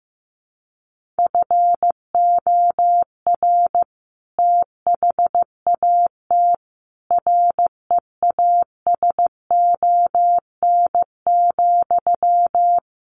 Morse code, heard as a keyed tone, about 15 wpm, 700 Hz. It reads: FOR THAT REASON,
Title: Sentences from Top 1500 Words 15wpm